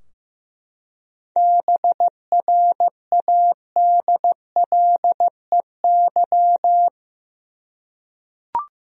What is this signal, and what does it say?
Morse code, heard as a keyed tone, about 15 wpm, 700 Hz. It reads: BRADLEY
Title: USA Common Names - 15wpm